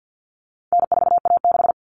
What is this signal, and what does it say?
Morse code, heard as a keyed tone, about 49 wpm, 700 Hz. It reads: N4A6
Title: Common Cut Numbers - 4-Digit 50wpm